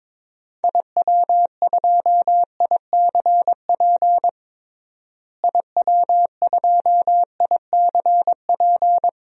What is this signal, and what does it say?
Morse code, heard as a keyed tone, about 22 wpm, 700 Hz. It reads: IW2ICP IW2ICP
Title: DXCC-100 Call Signs - 6 Characters - Repeated 2x - 22wpm